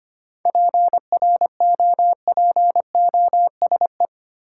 Morse code, heard as a keyed tone, about 25 wpm, 700 Hz. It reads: PROPOSE
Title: Seven Letter Words - 25wpm